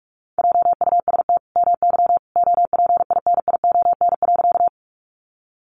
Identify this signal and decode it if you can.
Morse code, heard as a keyed tone, about 44 wpm, 700 Hz. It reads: JUST MY OPINION.
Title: Sentences from Top 1200 Words 45wpm